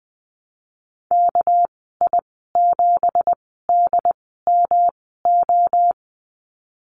Morse code, heard as a keyed tone, about 20 wpm, spaced as for 15 wpm, 700 Hz. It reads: KI7DMO
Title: 231 Lesson 37g - Letter x - Call Signs 20wpm